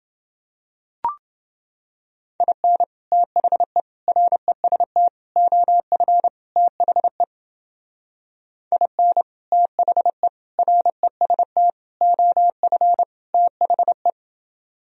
Morse code, heard as a keyed tone, about 30 wpm, 700 Hz. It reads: IN THE REST OF THE IN THE REST OF THE